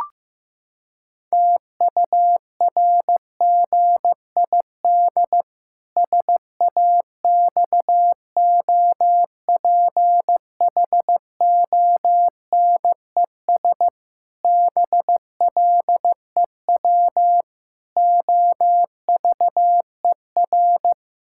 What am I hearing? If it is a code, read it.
Morse code, heard as a keyed tone, about 15 wpm, 700 Hz. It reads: TURGID SAXOPHONES BLEW OVER
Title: Pangram Challenge v2 - 15wpm